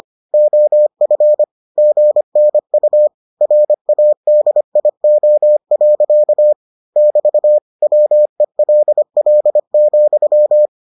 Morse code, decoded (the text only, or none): OF GNU RADIO. <BT> WELL,